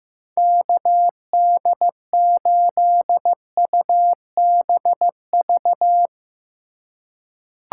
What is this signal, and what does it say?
Morse code, heard as a keyed tone, about 15 wpm, 700 Hz. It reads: KD8UBV
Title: USA Call Signs - 6 Characters - 2x Character Spacing - 15wpm